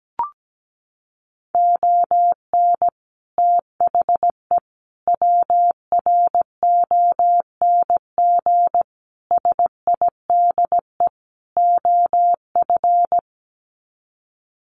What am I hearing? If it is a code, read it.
Morse code, heard as a keyed tone, about 17 wpm, 700 Hz. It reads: ON THE WRONG SIDE OF
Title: Sets of 5 Words Encore 17wpm